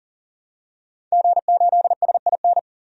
Morse code, heard as a keyed tone, about 40 wpm, 700 Hz. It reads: G8SIN
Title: DXCC-100 Call Signs - 5 Characters - 40wpm